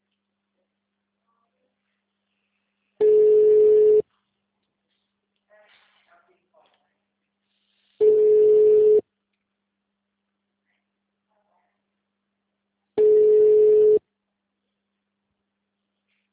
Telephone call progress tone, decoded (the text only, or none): ringback tone